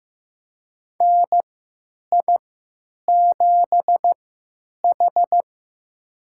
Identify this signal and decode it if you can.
Morse code, heard as a keyed tone, about 15 wpm, spaced as for 9 wpm, 700 Hz. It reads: NI7H
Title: USA Call Signs - 4 Characters - 3x Character Spacing - 15wpm